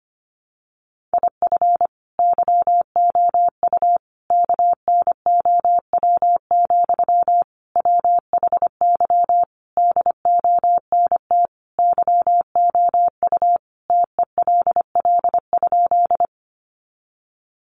Morse code, heard as a keyed tone, about 25 wpm, 700 Hz. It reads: IF YOU KNOW, WHY DONT YOU TELL?
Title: Sentences from Top 300 Words 25wpm